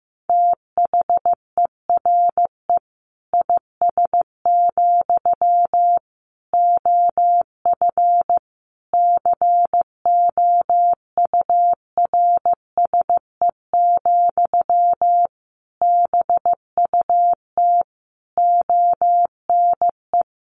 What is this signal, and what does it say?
Morse code, heard as a keyed tone, about 15 wpm, 700 Hz. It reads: THERE IS, OF COURSE, BUT ONE
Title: Sentences from Top 1200 Words 15wpm